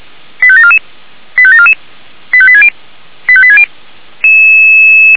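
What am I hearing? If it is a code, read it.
Five-tone selective call (ZVEI), heard as quick runs of five tones, tones of about 70 ms. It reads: 85744 85744 85860 85860
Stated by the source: ZVEI five-tone call 85827 received on SDR, from the audio server of Feuerwehr Orlamünde.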